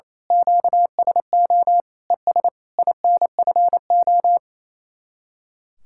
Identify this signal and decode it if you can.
Morse code, heard as a keyed tone, about 27 wpm, 700 Hz. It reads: QSO ES INFO
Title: Speed-Racing Common QSO Phrases - 28wpm